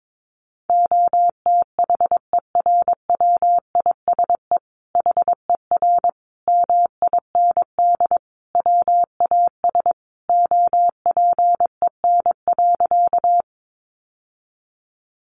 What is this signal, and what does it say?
Morse code, heard as a keyed tone, about 22 wpm, 700 Hz. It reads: OTHERWISE HER MIND WAS OPEN.
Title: Sentences from Top 1200 Words 22wpm